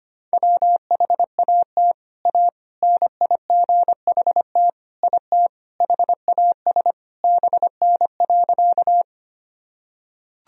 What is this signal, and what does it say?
Morse code, heard as a keyed tone, about 25 wpm, 700 Hz. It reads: WHAT A NIGHT IT HAS BN.
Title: Sentences from Top 300 Words and Abbreviations 25wpm